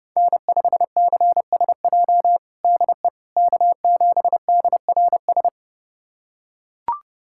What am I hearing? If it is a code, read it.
Morse code, heard as a keyed tone, about 30 wpm, 700 Hz. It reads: N5CSJ DE K7DRS
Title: USA Callsign DE Callsign - 5 Characters - 30wpm